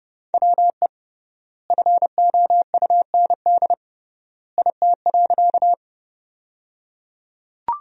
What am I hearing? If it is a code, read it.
Morse code, heard as a keyed tone, about 30 wpm, 700 Hz. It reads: WE FOUND IT.
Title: 3x Word-Spacing - Sentences from Top 300 Words - 30wpm